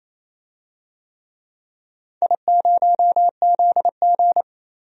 Morse code, heard as a keyed tone, about 28 wpm, 700 Hz. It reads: I0ZG
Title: DXCC-100 Call Signs - 4 Characters - 28wpm